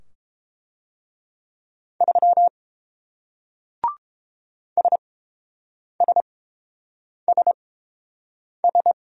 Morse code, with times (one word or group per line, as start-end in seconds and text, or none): 2.0-2.5 3
4.8-8.9 SSSS